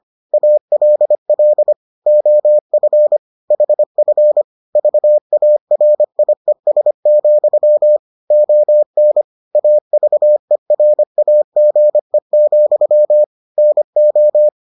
Morse code, heard as a keyed tone, about 25 wpm, 600 Hz. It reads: ALL OF HF VARIES, ON AVERAGE, NO